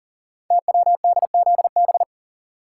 Morse code, heard as a keyed tone, about 42 wpm, 690 Hz. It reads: TWDZB
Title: Code Groups - 5 Letters - 40wpm